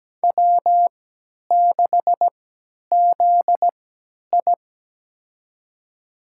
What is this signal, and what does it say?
Morse code, heard as a keyed tone, about 17 wpm, spaced as for 10 wpm, 700 Hz. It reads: W6ZI
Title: USA Call Signs - 4 Characters - 3x Character Spacing - 17wpm